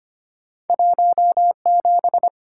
Morse code, heard as a keyed tone, about 33 wpm, 700 Hz. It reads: 17